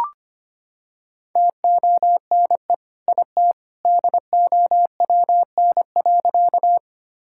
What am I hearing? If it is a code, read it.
Morse code, heard as a keyed tone, about 25 wpm, 700 Hz. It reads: TONE IT DOWN.
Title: Sentences from Top 1600 Words 25wpm